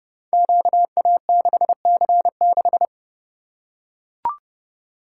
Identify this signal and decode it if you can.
Morse code, heard as a keyed tone, about 29 wpm, 700 Hz. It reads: QA6C6
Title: Code Groups - 5 Letter:Numbers - 30wpm